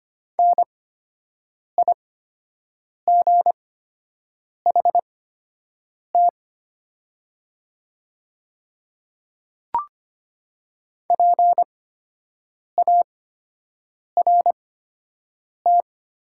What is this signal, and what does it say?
Morse code, heard as a keyed tone, about 25 wpm, spaced as for 7 wpm, 700 Hz. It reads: NIGHT PART
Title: 8x Character Spacing - Top 300 Words - 25wpm